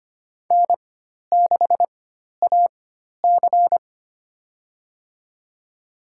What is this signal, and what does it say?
Morse code, heard as a keyed tone, about 25 wpm, spaced as for 12 wpm, 700 Hz. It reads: N6AC
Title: USA Call Signs - 4 Characters - 4x Character Spacing - 25wpm